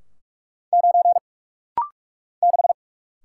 Morse code, heard as a keyed tone, about 46 wpm, 700 Hz. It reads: 9 6
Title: Single Letter-Number - Rapid-Fire 45wpm